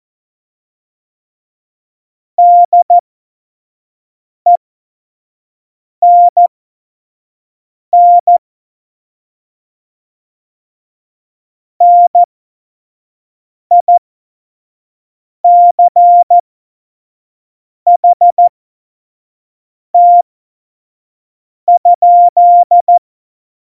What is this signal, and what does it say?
Morse code, heard as a keyed tone, about 14 wpm, spaced as for 5 wpm, 700 Hz. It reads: DENN NICHT?
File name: Morsekurs / Witze / Witz 18 August 2020, 25 F70